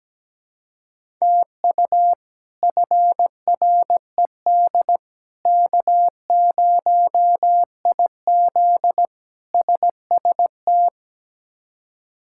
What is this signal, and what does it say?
Morse code, heard as a keyed tone, about 17 wpm, 700 Hz. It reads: TU FRED K0IZ SST
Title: K1USN SST QSO Practice - 17wpm